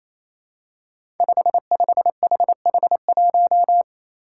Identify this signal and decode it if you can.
Morse code, heard as a keyed tone, about 27 wpm, 700 Hz. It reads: 55HH1
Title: Five Challenging Letter-Number Combinations 28wpm